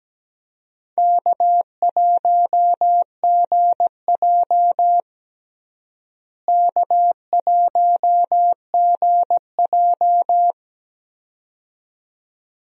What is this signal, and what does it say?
Morse code, heard as a keyed tone, about 17 wpm, 700 Hz. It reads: K1GJ K1GJ
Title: USA Call Signs - 4 Characters - Repeated 2x - 17wpm